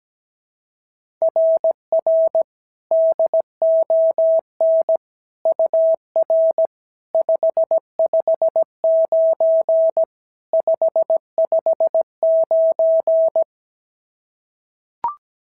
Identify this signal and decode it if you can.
Morse code, heard as a keyed tone, about 16 wpm, 650 Hz. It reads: RR DON UR 559 559
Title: SOTA QSO Practice 17wpm